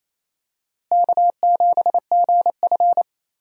Morse code, heard as a keyed tone, about 28 wpm, 700 Hz. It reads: K7GF
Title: USA Call Signs - 4 Characters - Repeated 3x - 28wpm